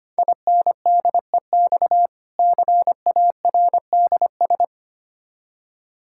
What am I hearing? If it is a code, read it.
Morse code, heard as a keyed tone, about 25 wpm, 700 Hz. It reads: INDEX CARDS
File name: practice / 25wpm / Difficult Words with Hints - 25wpm